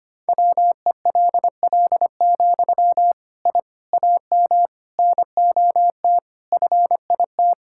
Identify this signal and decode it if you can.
Morse code, heard as a keyed tone, about 25 wpm, 700 Hz. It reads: WELL, I AM NOT FIT
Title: Sentences from Top 1000 Words - Review Part 5 - 25wpm